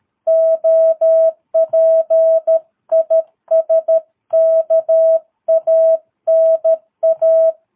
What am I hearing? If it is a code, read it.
Morse code, heard as a keyed tone, about 13 wpm, 660 Hz. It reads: OP IS KANA
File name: jh8cbh / 2022cw